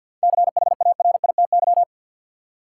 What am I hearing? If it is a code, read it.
Morse code, heard as a keyed tone, about 53 wpm, 700 Hz. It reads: CHARITY
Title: Seven Letter Words - 50wpm